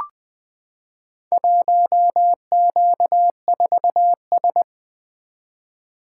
Morse code, heard as a keyed tone, about 20 wpm, 700 Hz. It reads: 1Q4S
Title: Four Letter-Number Combinations - 20wpm